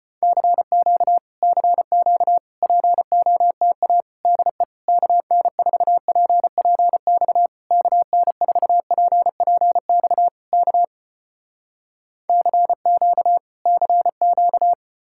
Morse code, heard as a keyed tone, about 36 wpm, 700 Hz. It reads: CQ CQ POTA DE KN4PPX KN4PPX K CQ CQ